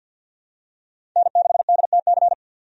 Morse code, heard as a keyed tone, about 52 wpm, 700 Hz. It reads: N6DTC
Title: USA Call Signs - 5 Characters - 50wpm